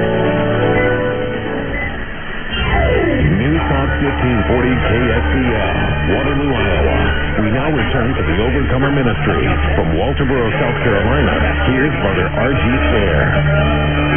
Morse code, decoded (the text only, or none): none